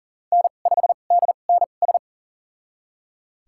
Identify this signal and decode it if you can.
Morse code, heard as a keyed tone, about 40 wpm, spaced as for 29 wpm, 700 Hz. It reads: N5DNS
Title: USA Call Signs - 5 Characters - 2x Character Spacing - 40wpm